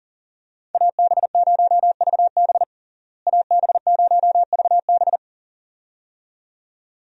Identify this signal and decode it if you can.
Morse code, heard as a keyed tone, about 37 wpm, 700 Hz. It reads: AB0VB AB0VB
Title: USA Call Signs - 5 Characters - Repeated 2x - 40wpm